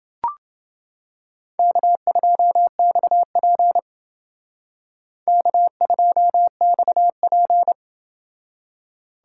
Morse code, heard as a keyed tone, about 28 wpm, 700 Hz. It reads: K2XP K2XP